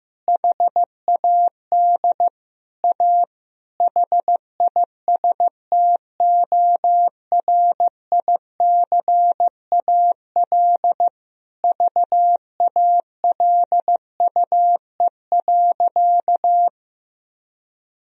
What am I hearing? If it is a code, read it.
Morse code, heard as a keyed tone, about 15 wpm, 700 Hz. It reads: HAD A HISTORICAL VALUE.
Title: Sentences from Top 1000 Words - Review Part 4 - 15wpm